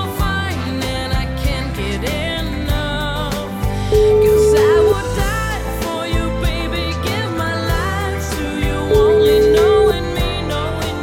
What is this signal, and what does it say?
Ringback tone (Europe, 425 Hz), with noise.